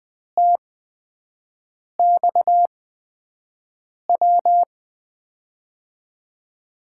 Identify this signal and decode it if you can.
Morse code, heard as a keyed tone, about 20 wpm, spaced as for 5 wpm, 700 Hz. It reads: TXW
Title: USA Call Signs - 6 Characters - 8x Character Spacing - 20wpm